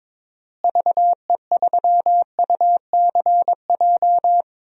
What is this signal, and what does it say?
Morse code, heard as a keyed tone, about 22 wpm, 700 Hz. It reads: VE3UCJ
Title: DXCC-100 Call Signs - 6 Characters - Repeated 3x - 22wpm